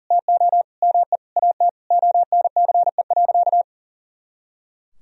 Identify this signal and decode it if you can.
Morse code, heard as a keyed tone, about 40 wpm, 700 Hz. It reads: TO ME AT ONCE.
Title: Sentences from Top 500 Words and Abbreviations 40wpm